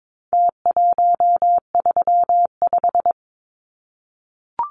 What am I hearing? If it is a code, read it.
Morse code, heard as a keyed tone, about 22 wpm, 700 Hz. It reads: T135
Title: Numbers - Contest Style - 4-Digit 22wpm